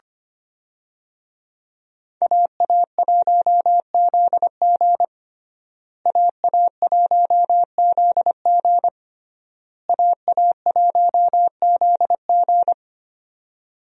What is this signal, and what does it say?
Morse code, heard as a keyed tone, about 25 wpm, 700 Hz. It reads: AA1ZG AA1ZG AA1ZG
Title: USA Call Signs - 5 Characters - Repeated 3x - 25wpm